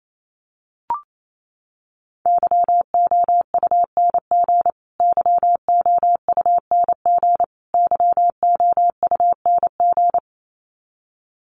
Morse code, heard as a keyed tone, about 28 wpm, 700 Hz. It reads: YOUNG YOUNG YOUNG
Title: Top 500 Words Repeated 28wpm